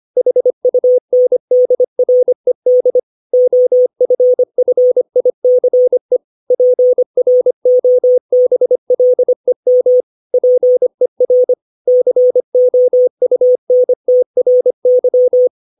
Morse code, heard as a keyed tone, about 25 wpm, 500 Hz. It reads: HUNDRED OFFICE PROBLEM PER COUNTRY
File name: 632 words / 632 Words 25WPM